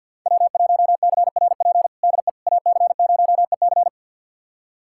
Morse code, heard as a keyed tone, about 52 wpm, 700 Hz. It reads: W1CRP DE AC0EC
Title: USA Callsign DE Callsign - 5 Characters - 50wpm